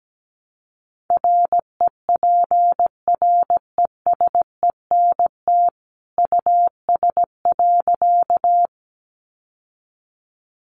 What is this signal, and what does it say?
Morse code, heard as a keyed tone, about 17 wpm, 700 Hz. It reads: REPRESENT US.
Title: Sentences from Top 1200 Words 17wpm